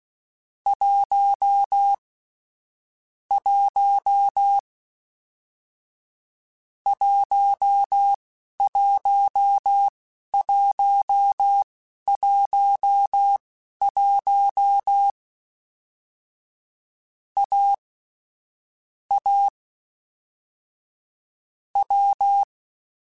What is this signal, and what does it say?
Morse code, heard as a keyed tone, about 16 wpm, spaced as for 11 wpm, 800 Hz. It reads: 1 1 11111 A A W